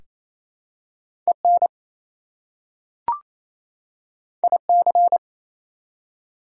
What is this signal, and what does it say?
Morse code, heard as a keyed tone, about 28 wpm, 700 Hz. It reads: EN IC
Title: Prefixes and Suffixes 28wpm